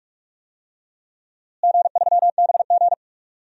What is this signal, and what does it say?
Morse code, heard as a keyed tone, about 44 wpm, 700 Hz. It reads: G3BG
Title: DXCC-100 Call Signs - 4 Characters - Repeated 2x - 45wpm